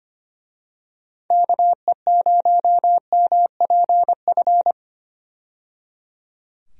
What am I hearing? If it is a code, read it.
Morse code, heard as a keyed tone, about 25 wpm, 700 Hz. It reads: KE0MPF
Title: USA Call Signs - 6 Characters - 25wpm